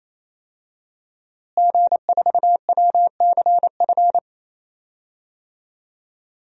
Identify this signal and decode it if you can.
Morse code, heard as a keyed tone, about 28 wpm, 700 Hz. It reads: G4WCF
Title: DXCC-100 Call Signs - 5 Characters - Repeated 3x - 28wpm